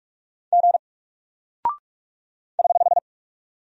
Morse code, heard as a keyed tone, about 44 wpm, 700 Hz. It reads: G <HH>
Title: Single Character (Letter:Number:Punctuation:Prosign) Alt 45wpm